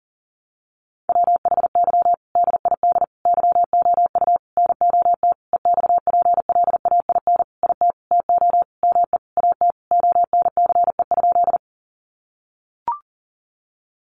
Morse code, heard as a keyed tone, about 37 wpm, 700 Hz. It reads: WHY DID YOU NOT EXPLAIN IT TO ME AT ONCE?
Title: Sentences from Top 1100 Words 40wpm